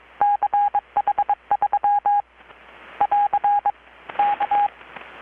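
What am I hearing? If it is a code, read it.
Morse code, heard as a keyed tone, about 23 wpm, 820 Hz. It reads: CH3 <AR> K
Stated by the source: Morse from KPH's Night of Nights XX maritime radiotelegraphy event, received in Chiapas, Mexico about 3,800 km away on a Kenwood R-600 with a grounded 10 m long-wire antenna.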